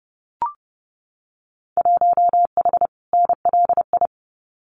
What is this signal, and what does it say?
Morse code, heard as a keyed tone, about 30 wpm, 700 Hz. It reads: 1H NLI
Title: Winter Field Day Exchanges 30wpm